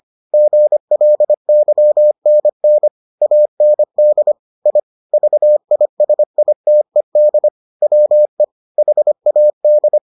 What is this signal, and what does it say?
Morse code, heard as a keyed tone, about 25 wpm, 600 Hz. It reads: GLYNN AND I VISITED WE HAD